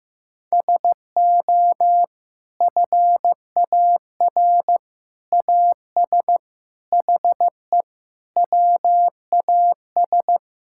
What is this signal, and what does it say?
Morse code, heard as a keyed tone, about 15 wpm, 700 Hz. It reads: SO FAR AS HE WAS
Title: Sentences from Top 1000 Words - Review Part 6 - 15wpm